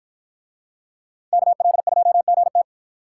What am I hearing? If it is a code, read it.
Morse code, heard as a keyed tone, about 53 wpm, 700 Hz. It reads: KL2GT